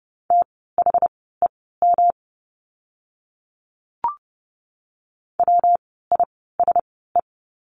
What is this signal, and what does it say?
Morse code, heard as a keyed tone, about 30 wpm, spaced as for 17 wpm, 700 Hz. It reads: THEM WISE
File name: course / 30:10wpm / 115 Lesson 19f - Letter w - Words 30wpm